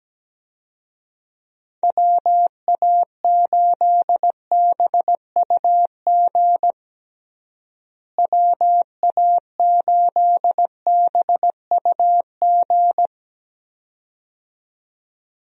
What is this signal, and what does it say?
Morse code, heard as a keyed tone, about 17 wpm, 700 Hz. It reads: WA8BUG WA8BUG
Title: USA Call Signs - 6 Characters - Repeated 2x - 17wpm